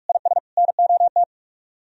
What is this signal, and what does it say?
Morse code, heard as a keyed tone, about 46 wpm, 700 Hz. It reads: IS NOT
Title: Speed-Racing - Sets of 3 Words 45wpm